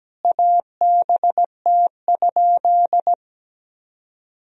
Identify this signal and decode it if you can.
Morse code, heard as a keyed tone, about 17 wpm, 700 Hz. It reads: ABT?
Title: Sentences from Top 100 Words and Abbreviations 17wpm